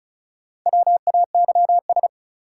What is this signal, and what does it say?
Morse code, heard as a keyed tone, about 36 wpm, 700 Hz. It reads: WAYS
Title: Sets of 5 Words Encore - New Word Drill - 35wpm